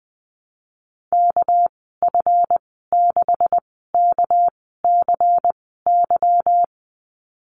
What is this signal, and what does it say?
Morse code, heard as a keyed tone, about 19 wpm, spaced as for 14 wpm, 700 Hz. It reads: KF6KCY
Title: 238 Lesson 38g - Letter q - Call Signs 20wpm